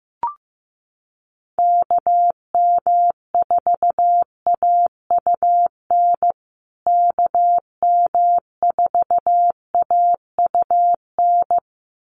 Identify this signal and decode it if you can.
Morse code, heard as a keyed tone, about 15 wpm, 700 Hz. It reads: KM4AUN KM4AUN
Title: USA Call Signs Repeated 2x 15wpm